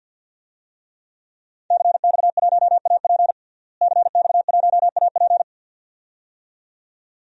Morse code, heard as a keyed tone, about 52 wpm, 700 Hz. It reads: CX1AP CX1AP
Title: DXCC-100 Call Signs - 5 Characters - Repeated 2x - 50wpm